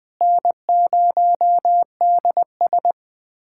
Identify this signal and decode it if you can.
Morse code, heard as a keyed tone, about 20 wpm, 700 Hz. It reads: N0DS
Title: USA Callsign DE Callsign - 4 Characters - 20wpm